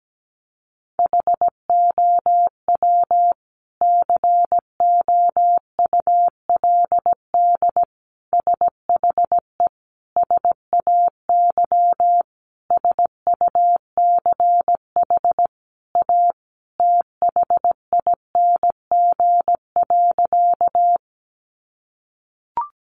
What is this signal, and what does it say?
Morse code, heard as a keyed tone, about 17 wpm, 700 Hz. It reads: HOW COULD SHE SAY SUCH A THING.
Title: Sentences from Top 200 Words 17wpm